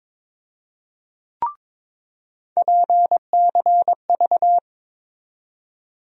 Morse code, heard as a keyed tone, about 22 wpm, 700 Hz. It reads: PCV
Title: USA Call Sign Postfixes - 3 Letters 22wpm